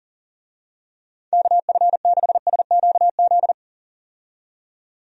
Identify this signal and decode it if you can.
Morse code, heard as a keyed tone, about 41 wpm, 700 Hz. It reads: KF6SQZ
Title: USA Call Signs - 6 Characters - 40wpm